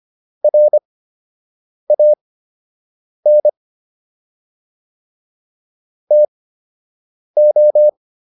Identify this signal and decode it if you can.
Morse code, heard as a keyed tone, about 25 wpm, spaced as for 7 wpm, 600 Hz. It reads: RAN TO